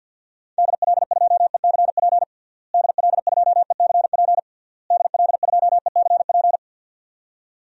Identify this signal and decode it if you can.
Morse code, heard as a keyed tone, about 52 wpm, 700 Hz. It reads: DL2ECP DL2ECP DL2ECP
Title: DXCC-100 Call Signs - 6 Characters - Repeated 3x - 50wpm